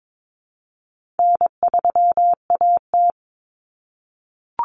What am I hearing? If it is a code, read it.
Morse code, heard as a keyed tone, about 22 wpm, 700 Hz. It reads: N3AT
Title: Common Cut Numbers - 4-Digit 22wpm